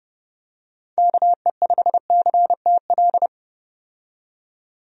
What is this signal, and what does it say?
Morse code, heard as a keyed tone, about 32 wpm, 700 Hz. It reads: KE5CTL
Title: USA Call Signs - 6 Characters - 30wpm